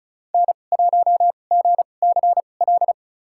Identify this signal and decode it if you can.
Morse code, heard as a keyed tone, about 36 wpm, spaced as for 25 wpm, 700 Hz. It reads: N1GCL
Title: USA Call Signs - 5 Characters - 2x Character Spacing - 35wpm